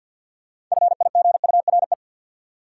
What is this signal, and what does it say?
Morse code, heard as a keyed tone, about 53 wpm, 700 Hz. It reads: FIGURE
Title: Sentences from Top 1000 Words - Review Part 2 - New Word Drill - 50wpm